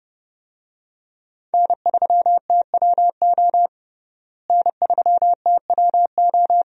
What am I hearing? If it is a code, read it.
Morse code, heard as a keyed tone, about 30 wpm, 700 Hz. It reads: N3TWO N3TWO
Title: USA Call Signs - 5 Characters - Repeated 2x - 30wpm